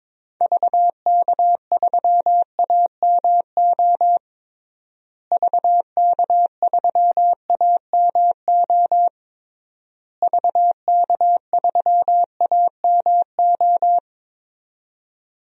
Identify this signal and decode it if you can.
Morse code, heard as a keyed tone, about 22 wpm, 700 Hz. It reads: VK3AMO VK3AMO VK3AMO
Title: DXCC-100 Call Signs - 6 Characters - Repeated 3x - 22wpm